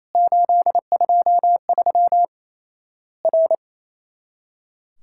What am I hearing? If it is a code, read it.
Morse code, heard as a keyed tone, about 28 wpm, 700 Hz. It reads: 823 R
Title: Worked All Europe DX Contest - QTC Exchange - 28wpm